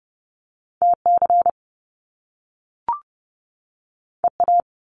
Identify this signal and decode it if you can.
Morse code, heard as a keyed tone, about 30 wpm, 700 Hz. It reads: TC EA
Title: Top Two Letter Combinations in Words 30wpm